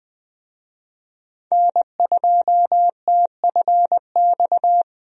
Morse code, heard as a keyed tone, about 20 wpm, 700 Hz. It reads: N2TFX
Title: USA Call Signs - 5 Characters - Repeated 2x - 20wpm